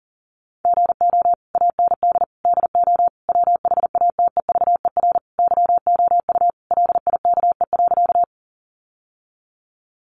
Morse code, heard as a keyed tone, about 40 wpm, 700 Hz. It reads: GO AND DO WHATEVER YOU LIKE.
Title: Sentences from Top 1100 Words 40wpm